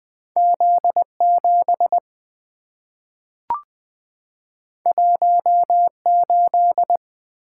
Morse code, heard as a keyed tone, about 20 wpm, 700 Hz. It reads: Z7 18
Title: Two Letter-Number Combinations - 20wpm